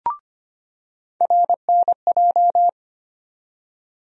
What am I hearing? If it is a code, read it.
Morse code, heard as a keyed tone, about 25 wpm, 700 Hz. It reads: RNJ
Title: USA Call Sign Postfixes - 3 Letters 25wpm